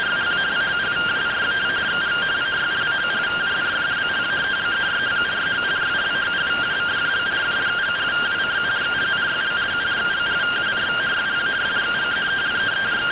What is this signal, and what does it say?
Two-tone teleprinter signal, about 170 Hz shift.